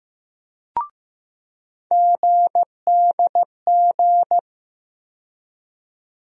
Morse code, heard as a keyed tone, about 15 wpm, 700 Hz. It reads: GDG
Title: USA Call Sign Postfixes - 3 Letters 15wpm